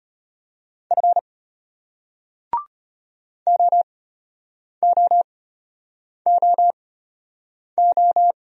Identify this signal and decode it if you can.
Morse code, heard as a keyed tone, about 32 wpm, spaced as for 8 wpm, 700 Hz. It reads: F OOOO